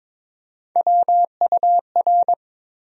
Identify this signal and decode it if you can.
Morse code, heard as a keyed tone, about 22 wpm, 700 Hz. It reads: WUR
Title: USA Call Sign Postfixes - 3 Letters 22wpm